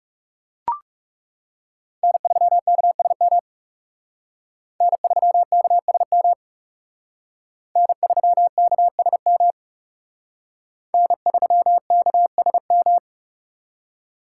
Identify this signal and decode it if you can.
Morse code, heard as a keyed tone, about 37 wpm, 700 Hz. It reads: N3KSM N3KSM N3KSM N3KSM
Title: Speed Racing - USA 1x3 Call Signs 30wpm